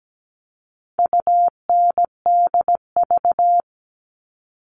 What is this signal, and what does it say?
Morse code, heard as a keyed tone, about 17 wpm, 700 Hz. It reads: UNDV
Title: All Cut Numbers - 4-Digit 17wpm